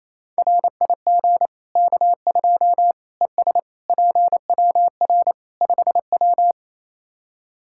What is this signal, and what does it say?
Morse code, heard as a keyed tone, about 27 wpm, 700 Hz. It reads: RIG K2 ES PWR 5W
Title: Practice QSO - Exchange 2 - Info - 28wpm